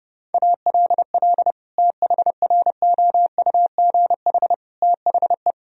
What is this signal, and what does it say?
Morse code, heard as a keyed tone, about 30 wpm, 700 Hz. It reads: ALL THROUGH THE
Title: Sentences from Top 1000 Words - Review Part 2 - 30wpm